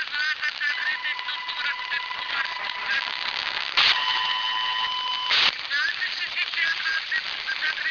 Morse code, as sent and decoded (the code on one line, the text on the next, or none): none